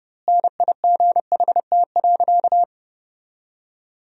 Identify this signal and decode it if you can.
Morse code, heard as a keyed tone, about 30 wpm, 700 Hz. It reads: NIGHT.
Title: 2x Word-Spacing - Sentences from Top 300 Words - 30wpm